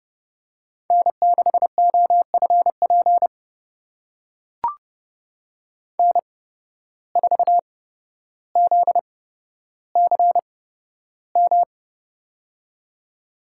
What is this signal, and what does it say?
Morse code, heard as a keyed tone, about 29 wpm, spaced as for 13 wpm, 700 Hz. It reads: N6OFP N4ZCM
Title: USA Call Signs - 5 Characters - 8x Character Spacing - 30wpm